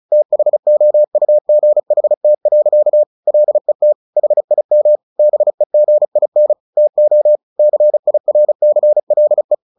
Morse code, heard as a keyed tone, about 34 wpm, 600 Hz. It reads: THOUGHT. LET HIM BEGIN TO CIRCLE
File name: the old man and the sea 2 35wpm